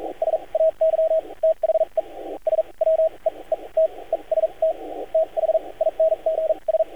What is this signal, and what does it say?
Morse code, heard as a keyed tone, about 42 wpm, spaced as for 30 wpm, 640 Hz, with radio noise. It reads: SAY THE SWEETEST THINGS
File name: archived / CRSnet